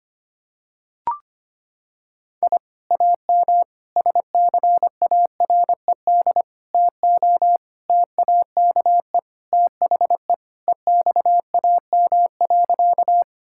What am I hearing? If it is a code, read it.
Morse code, heard as a keyed tone, about 25 wpm, 700 Hz. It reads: I AM SCARED TO TAKE THE EXAM.
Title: Sentences from Top 1000 Words - Review Part 5 - 25wpm